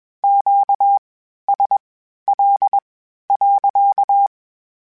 Morse code, heard as a keyed tone, about 21 wpm, spaced as for 12 wpm, 800 Hz. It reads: QSL.
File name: Q Calls / Q Calls-mp3s / Q Calls sppc @20